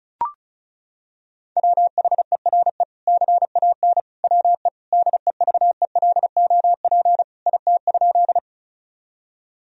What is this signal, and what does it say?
Morse code, heard as a keyed tone, about 34 wpm, 700 Hz. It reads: WHERE CAN WE DEVELOP IT?
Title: Sentences from Top 1000 Words - Review Part 2 - 35wpm